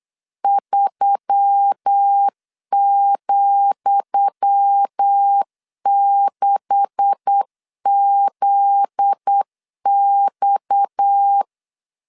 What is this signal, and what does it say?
Morse code, heard as a keyed tone, about 8 wpm, 800 Hz. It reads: 3,6ZX
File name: LEZIONI / 10 SIGNS / 10D COMA 8wpm 3749